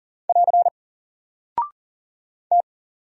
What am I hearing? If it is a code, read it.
Morse code, heard as a keyed tone, about 41 wpm, 700 Hz. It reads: <AR> T
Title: Single Character (Letter:Number:Punctuation:Prosign) 40wpm